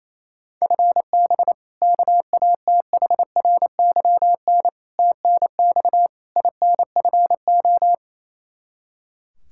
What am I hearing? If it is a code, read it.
Morse code, heard as a keyed tone, about 28 wpm, 700 Hz. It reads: FB KATHRYN TNX INFO
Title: Practice QSO - Exchange 3 - More Info - 28wpm